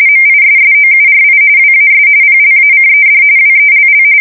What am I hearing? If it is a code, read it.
RTTY, about 45.45 baud. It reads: MISSOURI WAS PLACED WELL
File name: fsk170 / USS-Missouri